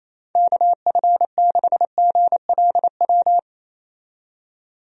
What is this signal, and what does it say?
Morse code, heard as a keyed tone, about 28 wpm, 700 Hz. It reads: KF6GLW
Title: USA Call Signs - 6 Characters - 28wpm